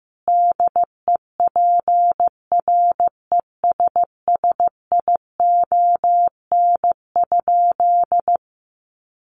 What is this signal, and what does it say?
Morse code, heard as a keyed tone, about 15 wpm, 700 Hz. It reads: DEPRESSION?
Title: Sentences from Top 1300 Words 15wpm